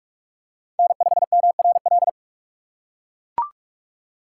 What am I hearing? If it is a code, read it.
Morse code, heard as a keyed tone, about 42 wpm, 700 Hz. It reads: N5MRL
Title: USA Call Signs - 5 Characters - 3x Character Spacing - 45wpm